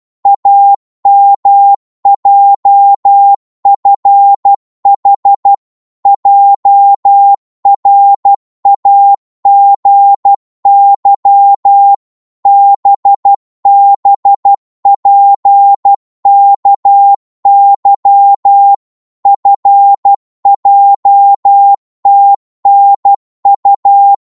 Morse code, heard as a keyed tone, about 12 wpm, 800 Hz. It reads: AMJFH JRAGY BBPKY FJTNU
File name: cwmp3 / ltr-12wpm